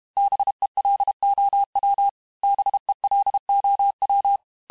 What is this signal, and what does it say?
Morse code, heard as a keyed tone, about 32 wpm, 800 Hz. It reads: BELOW BELOW
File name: cours-du-mercredi / audios / cours-868-teste-MOTS-vitesse-30mpm